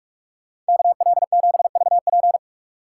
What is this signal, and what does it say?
Morse code, heard as a keyed tone, about 57 wpm, 690 Hz. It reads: KL7VP